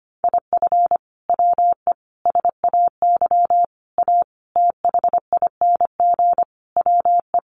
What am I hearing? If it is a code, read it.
Morse code, heard as a keyed tone, about 25 wpm, 700 Hz. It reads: IF WE SAY A THING WE
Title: Sentences from Top 1200 Words 25wpm